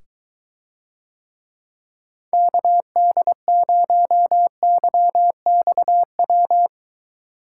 Morse code, heard as a keyed tone, about 23 wpm, 700 Hz. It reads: KD0YXW